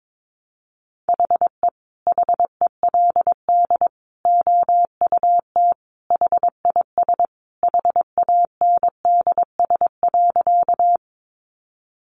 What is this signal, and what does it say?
Morse code, heard as a keyed tone, about 22 wpm, 700 Hz. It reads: HE HELD OUT HIS HANDS.
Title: A Sound Of Thunder 22wpm